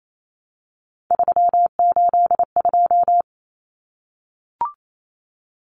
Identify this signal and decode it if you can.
Morse code, heard as a keyed tone, about 28 wpm, 700 Hz. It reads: 382
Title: Common Cut Numbers - 3-Digit 28wpm